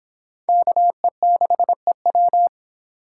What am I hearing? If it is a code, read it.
Morse code, heard as a keyed tone, about 25 wpm, 700 Hz. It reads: KE6EW